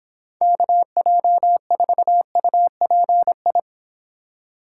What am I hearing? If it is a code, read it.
Morse code, heard as a keyed tone, about 26 wpm, 700 Hz. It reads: KJ4UPI